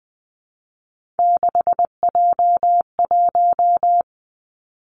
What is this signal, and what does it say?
Morse code, heard as a keyed tone, about 20 wpm, 700 Hz. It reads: 6J1
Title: Three Challenging Letter-Number Combinations 20wpm